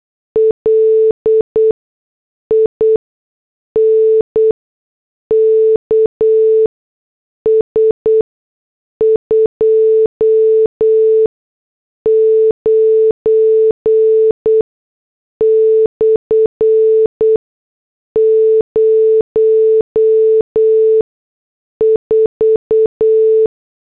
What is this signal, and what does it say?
Morse code, heard as a keyed tone, about 8 wpm, spaced as for 6 wpm, 440 Hz. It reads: LINKS 29/04
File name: morse / headlines